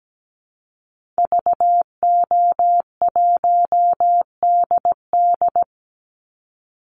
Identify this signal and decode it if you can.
Morse code, heard as a keyed tone, about 17 wpm, 700 Hz. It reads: VO1DD
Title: Call Signs - 17wpm